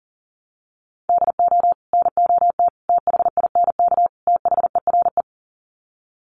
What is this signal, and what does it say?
Morse code, heard as a keyed tone, about 40 wpm, 700 Hz. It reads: DO NOT THINK THERE
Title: Sets of 4 Words 40wpm